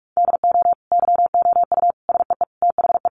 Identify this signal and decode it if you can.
Morse code, heard as a keyed tone, about 44 wpm, 700 Hz. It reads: DO YOU SEE THE
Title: Sentences from Top 1300 Words 45wpm